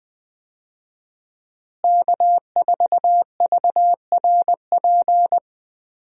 Morse code, heard as a keyed tone, about 20 wpm, 700 Hz. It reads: K4VRP
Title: USA Call Signs - 5 Characters - Repeated 2x - 20wpm